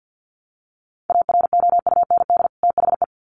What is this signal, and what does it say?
Morse code, heard as a keyed tone, about 50 wpm, 700 Hz. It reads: AROUND THE
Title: Sets of 2 Words 50wpm